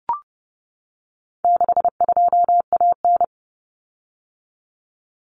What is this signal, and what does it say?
Morse code, heard as a keyed tone, about 30 wpm, 700 Hz. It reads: 62AN
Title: Common Cut Numbers - 4-Digit 30wpm